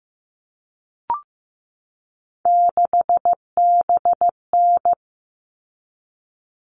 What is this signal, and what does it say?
Morse code, heard as a keyed tone, about 15 wpm, 700 Hz. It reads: 6BN
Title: All Cut Numbers - 3-Digit 15wpm